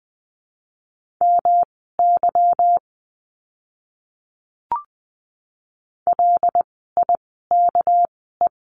Morse code, heard as a keyed tone, about 20 wpm, spaced as for 14 wpm, 700 Hz. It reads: MY LIKE
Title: 230 Lesson 37f - Letter x - Words 20wpm